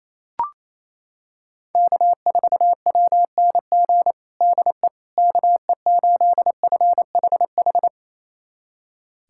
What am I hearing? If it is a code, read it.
Morse code, heard as a keyed tone, about 28 wpm, 700 Hz. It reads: K4WNG DE KE8FHH
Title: USA Callsign DE Callsign 28wpm